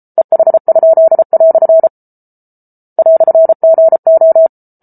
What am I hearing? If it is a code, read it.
Morse code, heard as a keyed tone, about 34 wpm, 650 Hz. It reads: EH?" "GO